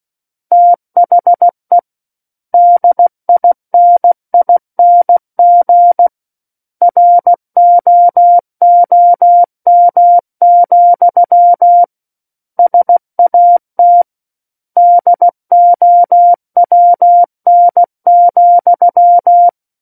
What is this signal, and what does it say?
Morse code, heard as a keyed tone, about 16 wpm, 700 Hz. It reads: THE DINING ROOM, SAT DOWN,
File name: Book One / 07-Chapter 7 - 16 WPM